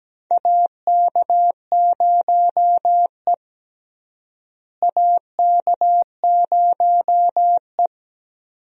Morse code, heard as a keyed tone, about 17 wpm, 700 Hz. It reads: AK0E AK0E
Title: USA Call Signs - 4 Characters - Repeated 3x - 17wpm